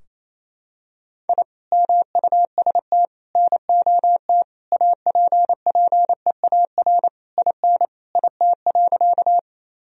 Morse code, heard as a keyed tone, about 28 wpm, 690 Hz. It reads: I MUST NOT APPEAR IN IT.
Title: Sentences from Top 1000 Words - Review Part 2 - 28wpm